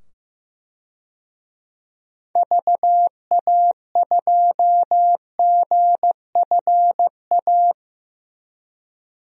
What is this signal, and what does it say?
Morse code, heard as a keyed tone, about 15 wpm, 700 Hz. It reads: VA2GFA
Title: DXCC-100 Call Signs - 6 Characters - 15wpm